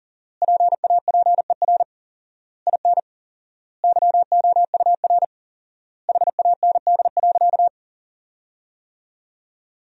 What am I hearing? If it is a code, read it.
Morse code, heard as a keyed tone, about 40 wpm, 700 Hz. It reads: PAPER IN YOUR HAND.
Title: 4x Word-Spacing - Sentences from Top 400 Words - 40wpm